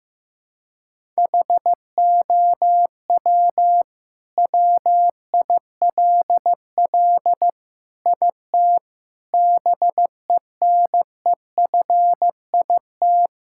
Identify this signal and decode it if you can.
Morse code, heard as a keyed tone, about 16 wpm, 700 Hz. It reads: HOW WILL IT BENEFIT
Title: Sentences from Top 1000 Words - Review Part 6 - 15wpm